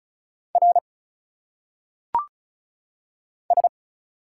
Morse code, heard as a keyed tone, about 36 wpm, 700 Hz. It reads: R S
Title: Single Letters - Phonetic Pronunciation 35wpm